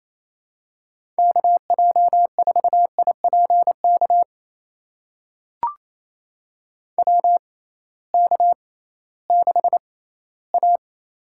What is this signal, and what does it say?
Morse code, heard as a keyed tone, about 28 wpm, spaced as for 16 wpm, 700 Hz. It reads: KJ4IPK WK6A
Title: USA Call Signs - 6x Character Spacing - 28wpm